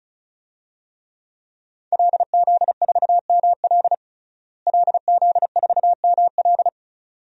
Morse code, heard as a keyed tone, about 36 wpm, 700 Hz. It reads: LZ4ML LZ4ML
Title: DXCC-100 Call Signs - 5 Characters - Repeated 2x - 35wpm